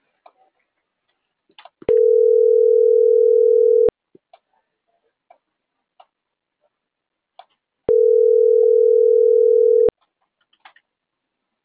Ringback tone (North America), 440 + 480 Hz.